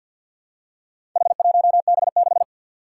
Morse code, heard as a keyed tone, about 53 wpm, 700 Hz. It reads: H1B6
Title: Four Challenging Letter-Number Combinations 50wpm